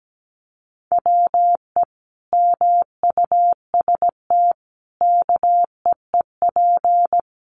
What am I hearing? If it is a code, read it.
Morse code, heard as a keyed tone, about 17 wpm, 700 Hz. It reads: WE MUST KEEP
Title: Sentences from Top 600 Words 17wpm